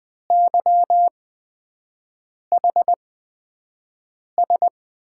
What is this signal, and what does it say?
Morse code, heard as a keyed tone, about 20 wpm, spaced as for 5 wpm, 700 Hz. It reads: YHS
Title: USA Call Signs - 5 Characters - 8x Character Spacing - 20wpm